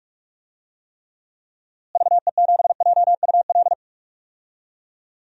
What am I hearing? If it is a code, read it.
Morse code, heard as a keyed tone, about 45 wpm, 700 Hz. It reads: VE7JUL
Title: DXCC-100 Call Signs - 6 Characters - Repeated 3x - 45wpm